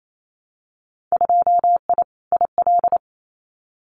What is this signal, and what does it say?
Morse code, heard as a keyed tone, about 28 wpm, 700 Hz. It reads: 2I IL
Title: Winter Field Day Exchanges 28wpm